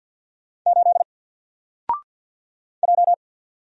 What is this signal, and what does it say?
Morse code, heard as a keyed tone, about 48 wpm, 700 Hz. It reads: 8 J
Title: Single Character (Letter:Number:Punctuation:Prosign) Alt 50wpm